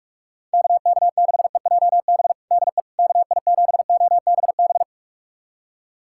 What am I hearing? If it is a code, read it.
Morse code, heard as a keyed tone, about 45 wpm, 690 Hz. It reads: KK6EJB DE KI7OBB
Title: USA Callsign DE Callsign - 6 Characters - 45wpm